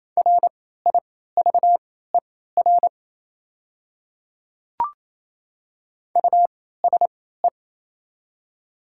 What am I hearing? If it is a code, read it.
Morse code, heard as a keyed tone, about 28 wpm, spaced as for 16 wpm, 700 Hz. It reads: RIVER USE
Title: 3x Character Spacing - Top 300 Words - 28wpm